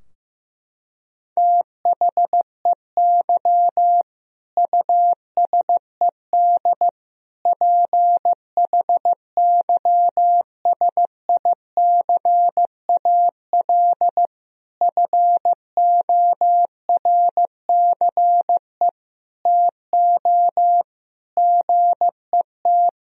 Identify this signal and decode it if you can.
Morse code, heard as a keyed tone, about 15 wpm, 700 Hz. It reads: THEY USED PHYSICAL FORCE TO GET
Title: Sentences from Top 1000 Words - Review Part 4 - 15wpm